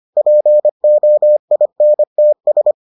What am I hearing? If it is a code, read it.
Morse code, heard as a keyed tone, about 25 wpm, 600 Hz. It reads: POINTS